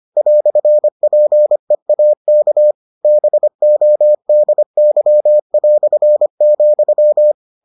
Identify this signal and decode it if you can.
Morse code, heard as a keyed tone, about 25 wpm, 600 Hz. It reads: "PEAK BODY",